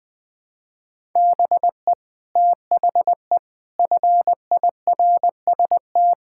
Morse code, heard as a keyed tone, about 20 wpm, 700 Hz. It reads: BE THE FIRST
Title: Sentences with 2-Word Phrases - 20wpm